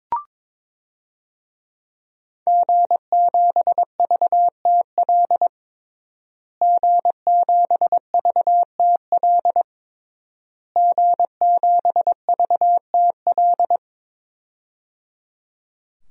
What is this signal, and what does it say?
Morse code, heard as a keyed tone, about 22 wpm, 700 Hz. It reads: G7VTL G7VTL G7VTL
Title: DXCC-100 Call Signs - 5 Characters - Repeated 3x - 22wpm